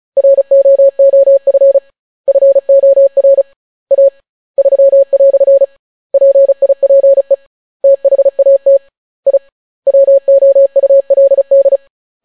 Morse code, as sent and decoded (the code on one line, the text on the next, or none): .-. --- --- ..-. / ..-. --- .-. / .- / ...-- .-..-. / .--. .. .--. / - .... .- - / .. / .-- --- ..- .-.. -..
ROOF FOR A 3" PIPE THAT I WOULD